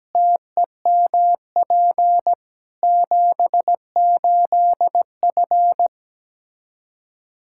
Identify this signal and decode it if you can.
Morse code, heard as a keyed tone, about 17 wpm, 700 Hz. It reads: TEMP 78F
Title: Practice QSO - Exchange 2 - Info - 17wpm